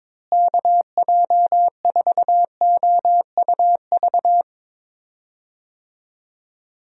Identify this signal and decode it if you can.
Morse code, heard as a keyed tone, about 22 wpm, 700 Hz. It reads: KJ4OUV
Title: USA Call Signs - 6 Characters - Repeated 3x - 22wpm